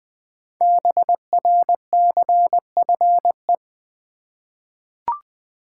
Morse code, heard as a keyed tone, about 20 wpm, 700 Hz. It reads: BRCFE
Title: Code Groups - 5 Letters - 20wpm